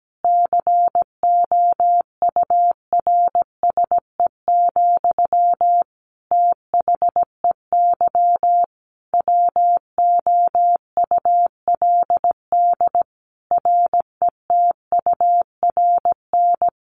Morse code, heard as a keyed tone, about 17 wpm, 700 Hz. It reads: COURSE, THEY WOULD RETURN
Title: Sentences from Top 1400 Words 17wpm